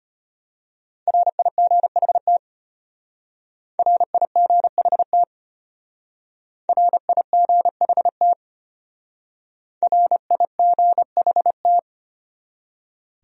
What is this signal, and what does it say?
Morse code, heard as a keyed tone, about 32 wpm, 700 Hz. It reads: RIGHT RIGHT RIGHT RIGHT